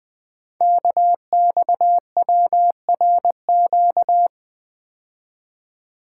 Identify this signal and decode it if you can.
Morse code, heard as a keyed tone, about 20 wpm, 700 Hz. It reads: KXWRQ
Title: Code Groups - 5 Letters - 20wpm